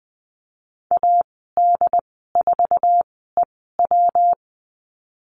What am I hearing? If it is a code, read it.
Morse code, heard as a keyed tone, about 20 wpm, spaced as for 15 wpm, 700 Hz. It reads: AD4EW
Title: 128 Lesson 21g - Number 6 - Call Signs 20wpm